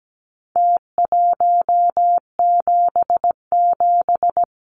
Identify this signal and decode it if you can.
Morse code, heard as a keyed tone, about 17 wpm, 700 Hz. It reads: T177
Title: Numbers - Contest Style - 4-Digit 17wpm